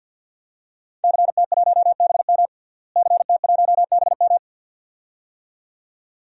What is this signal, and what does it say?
Morse code, heard as a keyed tone, about 51 wpm, 700 Hz. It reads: CT1BM CT1BM
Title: DXCC-100 Call Signs - 5 Characters - Repeated 2x - 50wpm